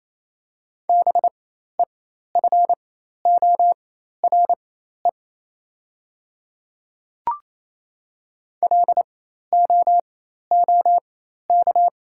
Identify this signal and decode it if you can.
Morse code, heard as a keyed tone, about 28 wpm, spaced as for 13 wpm, 700 Hz. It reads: BEFORE LOOK
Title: 4x Character Spacing - Top 200 Words - 28wpm